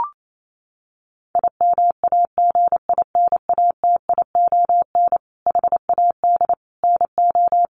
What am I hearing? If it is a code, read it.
Morse code, heard as a keyed tone, about 28 wpm, 700 Hz. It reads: IMAGINATION HAD NO
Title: Sentences from Top 1300 Words 28wpm